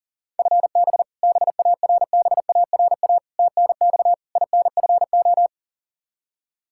Morse code, heard as a keyed tone, about 40 wpm, 700 Hz. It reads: FB BARBARA TNX INFO
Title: Practice QSO - Exchange 3 - More Info - 40wpm